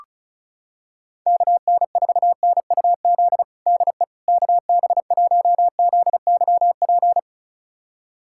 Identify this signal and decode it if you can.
Morse code, heard as a keyed tone, about 36 wpm, 700 Hz. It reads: KN4NUZ DE KB1ZYP
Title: USA Callsign DE Callsign 35wpm